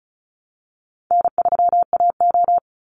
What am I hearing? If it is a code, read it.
Morse code, heard as a keyed tone, about 34 wpm, 700 Hz. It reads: N3AO
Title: Easy Call Signs Repeated 2x - 35wpm